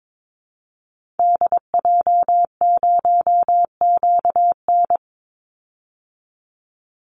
Morse code, heard as a keyed tone, about 22 wpm, 700 Hz. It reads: DJ0QN
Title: Call Signs Repeated 2x - 22wpm